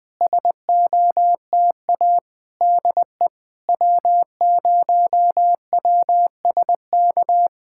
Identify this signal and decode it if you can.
Morse code, heard as a keyed tone, about 20 wpm, 700 Hz. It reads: SOTA DE W0WSK
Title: SOTA QSO Practice 20wpm